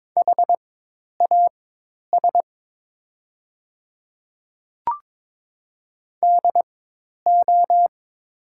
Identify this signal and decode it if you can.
Morse code, heard as a keyed tone, about 22 wpm, spaced as for 10 wpm, 700 Hz. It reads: HAS DO
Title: 4x Character Spacing - Top 100 Words - 22wpm